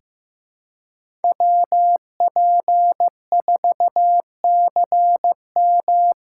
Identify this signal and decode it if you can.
Morse code, heard as a keyed tone, about 15 wpm, 700 Hz. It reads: WP4CM
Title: USA Call Signs - 3x Character Spacing - 15wpm